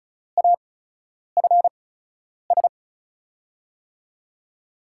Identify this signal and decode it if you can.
Morse code, heard as a keyed tone, about 36 wpm, spaced as for 10 wpm, 700 Hz. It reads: AFS
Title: USA Call Signs - 5 Characters - 8x Character Spacing - 35wpm